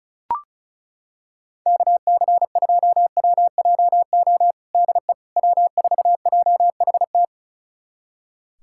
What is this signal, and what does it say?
Morse code, heard as a keyed tone, about 34 wpm, 700 Hz. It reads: KC2WJO DE W4JHT
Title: USA Callsign DE Callsign 35wpm